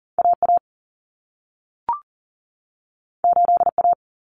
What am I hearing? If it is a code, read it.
Morse code, heard as a keyed tone, about 40 wpm, 700 Hz. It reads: AA 8A
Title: Common Cut Numbers - 2-Digit 40wpm